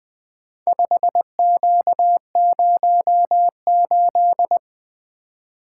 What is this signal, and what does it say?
Morse code, heard as a keyed tone, about 20 wpm, 700 Hz. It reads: 5Q08
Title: Four Letter-Number Combinations - 20wpm